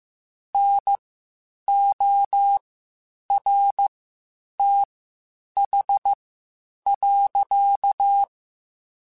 Morse code, heard as a keyed tone, about 15 wpm, spaced as for 8 wpm, 800 Hz. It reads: NORTH.
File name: Spaced Characters / Spaced Characters-mp3s / Weather 4 24 sp @13